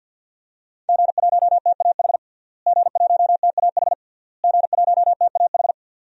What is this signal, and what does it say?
Morse code, heard as a keyed tone, about 46 wpm, 700 Hz. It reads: G1TAH G1TAH G1TAH
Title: DXCC-100 Call Signs - 5 Characters - Repeated 3x - 50wpm